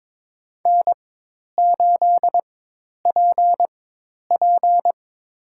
Morse code, heard as a keyed tone, about 21 wpm, spaced as for 10 wpm, 700 Hz. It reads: N8PP
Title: USA Call Signs - 4 Characters - 4x Character Spacing - 22wpm